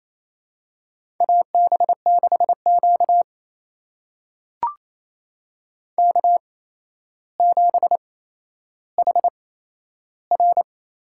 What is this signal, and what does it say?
Morse code, heard as a keyed tone, about 28 wpm, spaced as for 12 wpm, 700 Hz. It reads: AB6Q K7HR
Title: USA Call Signs - 8x Character Spacing - 28wpm